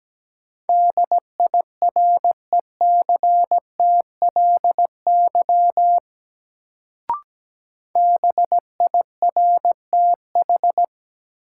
Morse code, heard as a keyed tone, about 17 wpm, 700 Hz. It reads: DIRECTLY BIRTH
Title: Sentences from Top 800 Words - New Word Drill - 17wpm